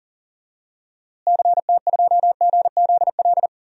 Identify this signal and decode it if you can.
Morse code, heard as a keyed tone, about 37 wpm, 700 Hz. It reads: CT2GZL
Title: DXCC-100 Call Signs - 6 Characters - Repeated 2x - 40wpm